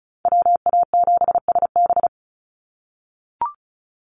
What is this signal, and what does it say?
Morse code, heard as a keyed tone, about 34 wpm, 700 Hz. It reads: WA7SB
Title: NA SOTA Chaser Callsigns 35wpm